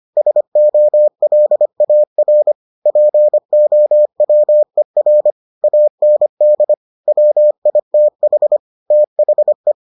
Morse code, heard as a keyed tone, about 25 wpm, 600 Hz. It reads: SOLAR POWER AND WITH THE